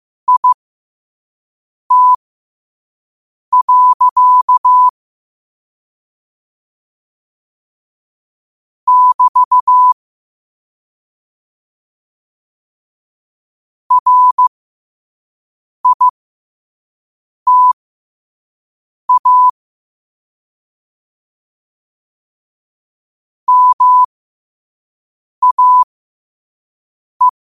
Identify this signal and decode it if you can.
Morse code, heard as a keyed tone, about 15 wpm, spaced as for 5 wpm, 1000 Hz. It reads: IT. <BT> RITA MAE
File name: qotdfiles / 5wpm